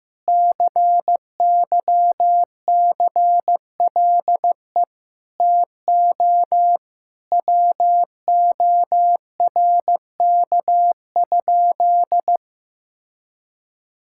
Morse code, heard as a keyed tone, about 15 wpm, 700 Hz. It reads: CYCLE TO WORK?
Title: Sentences from Top 1000 Words - Review Part 6 - 15wpm